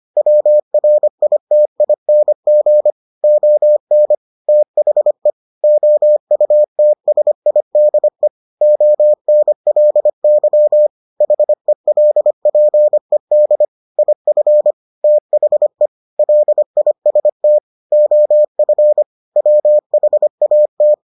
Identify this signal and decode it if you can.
Morse code, heard as a keyed tone, about 25 wpm, 600 Hz. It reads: WRITING ON THE OUTSIDE ONLY HELPED IF THE LIST OF WHAT